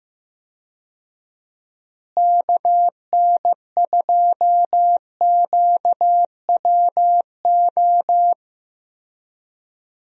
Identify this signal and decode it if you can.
Morse code, heard as a keyed tone, about 15 wpm, 700 Hz. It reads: KN2QWO
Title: USA Call Signs - 6 Characters - Repeated 2x - 15wpm